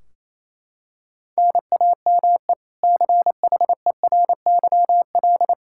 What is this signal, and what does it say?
Morse code, heard as a keyed tone, about 28 wpm, 700 Hz. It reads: NAME CHERYL
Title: K1USN SST QSO Practice 28wpm